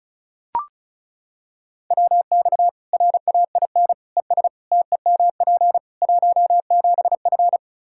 Morse code, heard as a keyed tone, about 36 wpm, 700 Hz. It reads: WX RAIN ES TEMP 17F
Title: Practice QSO - Exchange 2 - Info - 35wpm